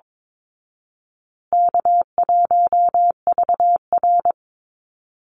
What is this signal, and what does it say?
Morse code, heard as a keyed tone, about 22 wpm, 700 Hz. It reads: K1VR
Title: Easy Call Signs Repeated 3x - 22wpm